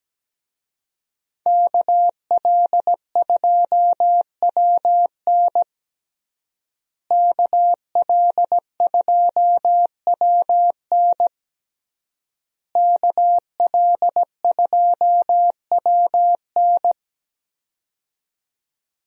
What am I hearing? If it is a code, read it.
Morse code, heard as a keyed tone, about 17 wpm, 700 Hz. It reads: KL2WN KL2WN KL2WN
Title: USA Call Signs - 5 Characters - Repeated 3x - 17wpm